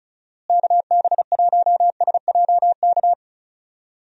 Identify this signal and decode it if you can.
Morse code, heard as a keyed tone, about 34 wpm, 700 Hz. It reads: KB1SJK
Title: USA Call Signs - 6 Characters - Repeated 2x - 35wpm